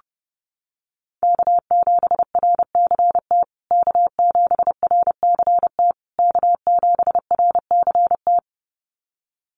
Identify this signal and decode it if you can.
Morse code, heard as a keyed tone, about 30 wpm, 700 Hz. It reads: K7RCT K7RCT K7RCT
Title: USA Call Signs Repeated 3x 30wpm